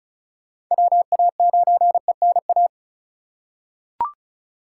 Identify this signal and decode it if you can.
Morse code, heard as a keyed tone, about 36 wpm, 700 Hz. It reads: WA9ENA
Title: K1USN SST QSO Practice - 35wpm